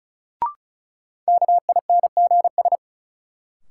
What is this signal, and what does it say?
Morse code, heard as a keyed tone, about 36 wpm, 700 Hz. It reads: KINGS
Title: Sentences from Top 1000 Words - Review Part 6 - New Word Drill - 35wpm